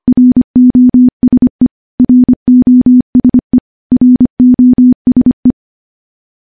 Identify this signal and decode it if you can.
Morse code, heard as a keyed tone, about 25 wpm, 250 Hz. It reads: ROSE ROSE ROSE
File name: morse / morse-training-25WPM